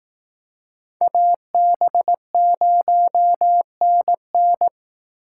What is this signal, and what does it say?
Morse code, heard as a keyed tone, about 18 wpm, 700 Hz. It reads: AB0NN
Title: Speed Racing - USA 2x2 Call Signs 15wpm